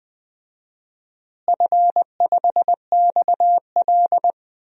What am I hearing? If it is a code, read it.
Morse code, heard as a keyed tone, about 21 wpm, 700 Hz. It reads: F5XL
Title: DXCC-100 Call Signs - 4 Characters - Repeated 3x - 20wpm